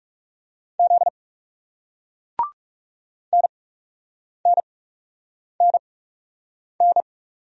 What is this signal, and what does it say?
Morse code, heard as a keyed tone, about 40 wpm, 700 Hz. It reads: Z N N N N